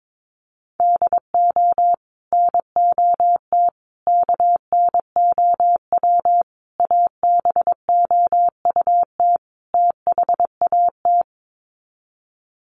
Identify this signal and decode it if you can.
Morse code, heard as a keyed tone, about 22 wpm, 700 Hz. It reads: DO NOT KNOW ABOUT THAT
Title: Sets of 5 Words Encore2 22wpm